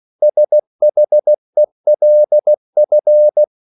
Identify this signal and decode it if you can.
Morse code, heard as a keyed tone, about 16 wpm, 600 Hz. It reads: SHELF
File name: CW / Practice / EnglishQuotations0216